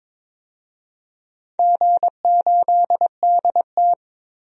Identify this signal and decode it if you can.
Morse code, heard as a keyed tone, about 22 wpm, 700 Hz. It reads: G8DT
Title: DXCC-100 Call Signs - 4 Characters - 22wpm